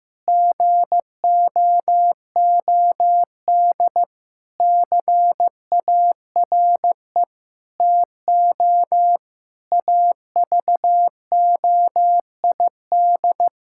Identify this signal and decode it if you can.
Morse code, heard as a keyed tone, about 15 wpm, 700 Hz. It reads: GOOD CARE TO AVOID
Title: Sentences from Top 1000 Words - Review Part 6 - 15wpm